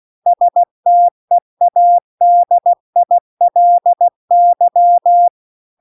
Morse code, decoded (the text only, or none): STEADILY